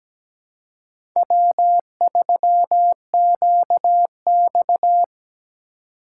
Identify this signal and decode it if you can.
Morse code, heard as a keyed tone, about 17 wpm, 700 Hz. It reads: W3QX
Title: USA Call Signs - 6x Character Spacing - 17wpm